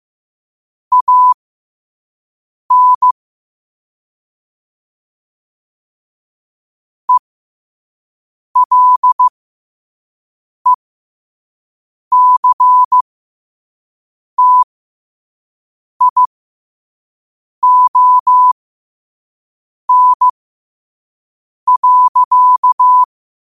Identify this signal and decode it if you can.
Morse code, heard as a keyed tone, about 15 wpm, spaced as for 5 wpm, 1000 Hz. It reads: AN ELECTION.
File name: qotdfiles / 5wpm